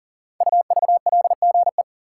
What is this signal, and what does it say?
Morse code, heard as a keyed tone, about 38 wpm, 700 Hz. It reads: UVLGE
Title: Code Groups - 5 Letter:Numbers - 40wpm